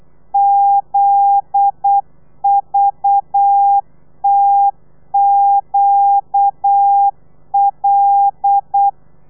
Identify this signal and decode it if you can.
Morse code, heard as a keyed tone, about 8 wpm, 800 Hz. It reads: ZVTQL
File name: telegrafia / i2tfj / Lez9 35